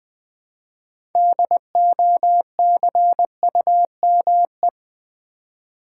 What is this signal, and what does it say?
Morse code, heard as a keyed tone, about 20 wpm, 700 Hz. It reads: DOCUME
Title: Build A Word - 2 Letters At A Time - 8-Letter Words - 20wpm